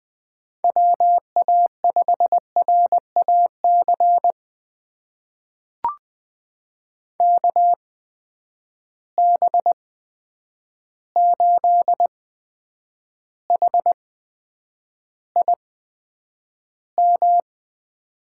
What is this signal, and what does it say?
Morse code, heard as a keyed tone, about 20 wpm, spaced as for 9 wpm, 700 Hz. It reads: WA5RAC KB8HIM
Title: USA Call Signs - 6 Characters - 8x Character Spacing - 20wpm